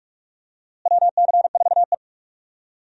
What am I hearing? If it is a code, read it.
Morse code, heard as a keyed tone, about 41 wpm, 700 Hz. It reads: WC4E
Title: USA Call Signs - 4 Characters - Repeated 3x - 45wpm